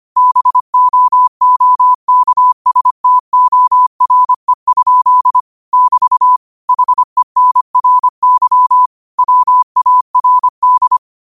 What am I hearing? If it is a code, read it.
Morse code, heard as a keyed tone, about 25 wpm, 1000 Hz. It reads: BOOKSTORE? <BT> HENRY WARD